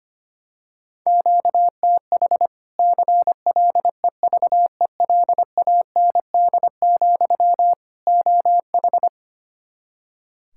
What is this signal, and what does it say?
Morse code, heard as a keyed tone, about 25 wpm, 700 Hz. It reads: QTH CLEVELAND, OH
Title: Practice QSO - Exchange 1 - Beginning - 25wpm